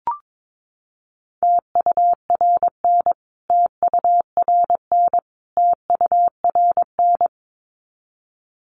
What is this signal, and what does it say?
Morse code, heard as a keyed tone, about 22 wpm, 700 Hz. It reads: TURN TURN TURN
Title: Top 500 Words Repeated 22wpm